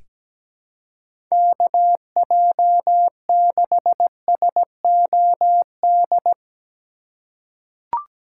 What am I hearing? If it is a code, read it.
Morse code, heard as a keyed tone, about 17 wpm, 700 Hz. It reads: KJ6SOD
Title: USA Call Signs - 6 Characters - 4x Character Spacing - 17wpm